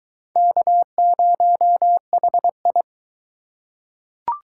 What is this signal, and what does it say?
Morse code, heard as a keyed tone, about 22 wpm, 700 Hz. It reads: K0HI